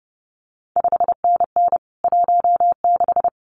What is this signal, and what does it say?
Morse code, heard as a keyed tone, about 30 wpm, 700 Hz. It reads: 5NN 16
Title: CQ WW DX Contest Exchange - 30wpm